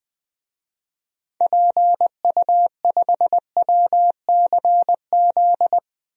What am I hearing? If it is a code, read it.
Morse code, heard as a keyed tone, about 21 wpm, 700 Hz. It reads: PU5WCZ
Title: DXCC-100 Call Signs - 6 Characters - Repeated 3x - 20wpm